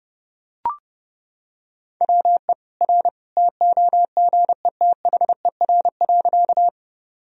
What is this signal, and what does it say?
Morse code, heard as a keyed tone, about 32 wpm, 700 Hz. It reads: WE R TOGETHER.
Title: Sentences from Top 300 Words and Abbreviations 30wpm